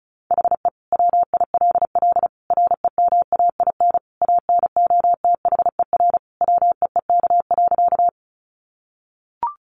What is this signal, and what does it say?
Morse code, heard as a keyed tone, about 34 wpm, 700 Hz. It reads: HE WILL REMAIN ANOTHER WEEK.
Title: Sentences from Top 1200 Words 35wpm